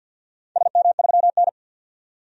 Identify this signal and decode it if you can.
Morse code, heard as a keyed tone, about 52 wpm, 700 Hz. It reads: SM3N
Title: DXCC-100 Call Signs - 4 Characters - 50wpm